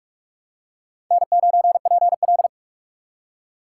Morse code, heard as a keyed tone, about 44 wpm, 700 Hz. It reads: N9PL
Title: USA Call Signs - 6x Character Spacing - 45wpm